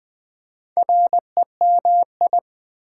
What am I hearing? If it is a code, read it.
Morse code, heard as a keyed tone, about 20 wpm, 700 Hz. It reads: REMI
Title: Build A Word - 2 Letters At A Time - 8-Letter Words - 20wpm